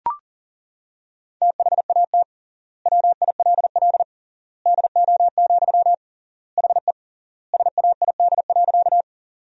Morse code, heard as a keyed tone, about 40 wpm, 700 Hz. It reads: THAT WILL DO, HE SAID.
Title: 3x Word-Spacing - Sentences from Top 100 Words - 40wpm